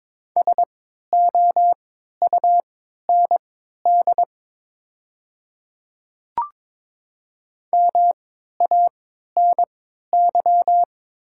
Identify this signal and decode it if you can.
Morse code, heard as a keyed tone, about 22 wpm, spaced as for 12 wpm, 700 Hz. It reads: SOUND MANY
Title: 3x Character Spacing - Top 100 Words - 22wpm